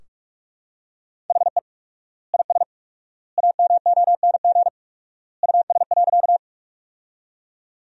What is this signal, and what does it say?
Morse code, heard as a keyed tone, about 41 wpm, 700 Hz. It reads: HE IS AMONG US.
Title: 4x Word-Spacing - Sentences from Top 500 Words - 45wpm